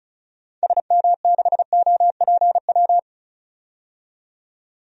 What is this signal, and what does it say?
Morse code, heard as a keyed tone, about 36 wpm, 690 Hz. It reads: SM6OPW
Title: DXCC-100 Call Signs - 6 Characters - 35wpm